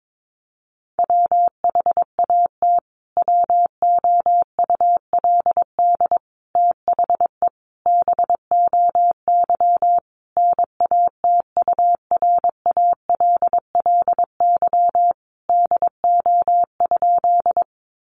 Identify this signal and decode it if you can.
Morse code, heard as a keyed tone, about 22 wpm, 700 Hz. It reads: WHAT WOULD THE BOY NATURALLY DO?
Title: Sentences from Top 1300 Words 22wpm